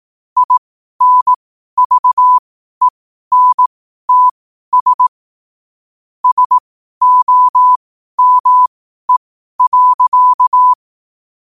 Morse code, heard as a keyed tone, about 18 wpm, spaced as for 12 wpm, 1000 Hz. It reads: INVENTS SOME.
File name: qotdfiles / 12wpm